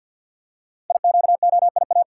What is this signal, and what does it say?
Morse code, heard as a keyed tone, about 48 wpm, 700 Hz. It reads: IQOIA
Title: Code Groups - 5 Letters - 50wpm